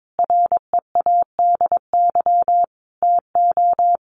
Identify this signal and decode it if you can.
Morse code, heard as a keyed tone, about 22 wpm, 700 Hz. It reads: READY TO
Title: Sentences from Top 1400 Words 22wpm